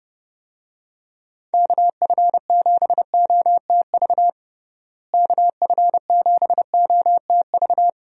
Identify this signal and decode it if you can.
Morse code, heard as a keyed tone, about 30 wpm, 700 Hz. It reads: KF7OTV KF7OTV
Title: USA Call Signs - 6 Characters - Repeated 2x - 30wpm